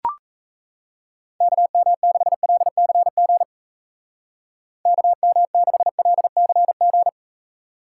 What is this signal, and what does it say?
Morse code, heard as a keyed tone, about 34 wpm, 700 Hz. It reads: KM6LCG KM6LCG
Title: Speed Racing - USA 2x3 Call Signs 28wpm